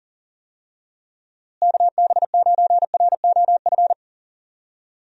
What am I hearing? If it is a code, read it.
Morse code, heard as a keyed tone, about 39 wpm, 700 Hz. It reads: KB9ROF
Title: USA Call Signs - 6 Characters - Repeated 3x - 40wpm